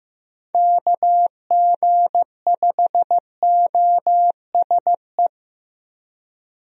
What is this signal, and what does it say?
Morse code, heard as a keyed tone, about 15 wpm, 700 Hz. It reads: KG5OSE
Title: USA Call Signs - 6 Characters - Repeated 2x - 15wpm